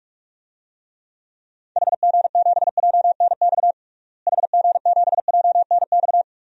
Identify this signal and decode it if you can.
Morse code, heard as a keyed tone, about 46 wpm, 700 Hz. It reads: HG7JNX HG7JNX
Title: DXCC-100 Call Signs - 6 Characters - Repeated 3x - 45wpm